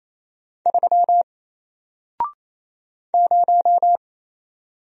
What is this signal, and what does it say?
Morse code, heard as a keyed tone, about 28 wpm, 700 Hz. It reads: 3 0
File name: Single-Letter-Number-28wpm